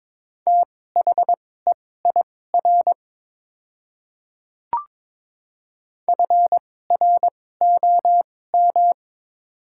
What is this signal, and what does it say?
Morse code, heard as a keyed tone, about 22 wpm, spaced as for 16 wpm, 700 Hz. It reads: THEIR FROM
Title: 2x Character Spacing - Top 200 Words - 22wpm